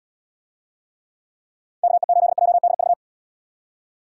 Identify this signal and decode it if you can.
Morse code, heard as a keyed tone, about 77 wpm, 700 Hz. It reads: KE8JDV